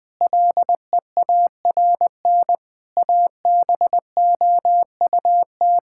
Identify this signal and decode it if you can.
Morse code, heard as a keyed tone, about 20 wpm, 700 Hz. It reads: LEARN ABOUT
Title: Sentences from Top 1000 Words - Review Part 6 - 20wpm